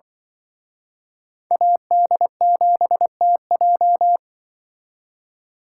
Morse code, heard as a keyed tone, about 24 wpm, 700 Hz. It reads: AD7TJ
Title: Speed Racing - USA 2x2 Call Signs 20wpm